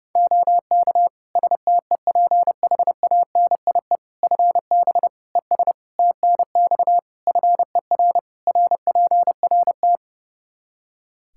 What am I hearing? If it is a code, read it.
Morse code, heard as a keyed tone, about 32 wpm, 700 Hz. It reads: OK STEPHANIE FB ES TNX FER RPRT
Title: Practice QSO - Exchange 2 - Info - 30wpm